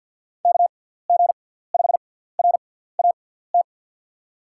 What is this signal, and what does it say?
Morse code, heard as a keyed tone, about 51 wpm, spaced as for 17 wpm, 700 Hz. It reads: KG5RAT
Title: USA Call Signs - 6 Characters - 6x Character Spacing - 50wpm